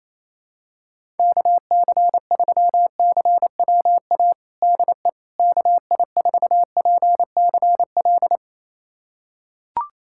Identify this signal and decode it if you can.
Morse code, heard as a keyed tone, about 28 wpm, 700 Hz. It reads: KC3CWA DE KI4PCL
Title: USA Callsign DE Callsign 28wpm